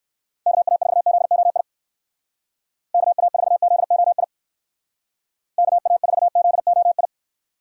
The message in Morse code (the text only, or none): KA4ZOI KA4ZOI KA4ZOI